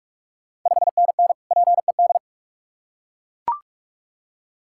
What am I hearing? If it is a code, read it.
Morse code, heard as a keyed tone, about 41 wpm, 700 Hz. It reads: 5NN PED
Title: Ontario QSO Party Exchange 45wpm